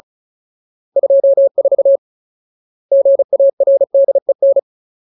Morse code, heard as a keyed tone, about 34 wpm, 550 Hz. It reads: 24 GARDEN
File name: Ultimate Word List / 35WPM Ultimate